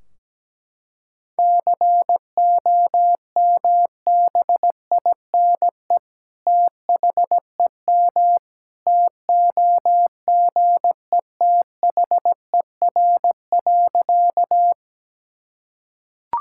Morse code, heard as a keyed tone, about 17 wpm, 700 Hz. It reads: COMBINE THEM TOGETHER.
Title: Sentences from Top 1600 Words 17wpm